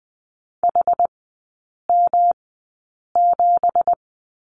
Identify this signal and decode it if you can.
Morse code, heard as a keyed tone, about 20 wpm, spaced as for 8 wpm, 700 Hz. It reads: HM7
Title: 178 Lesson 29e - Slash - Three Characters 20wpm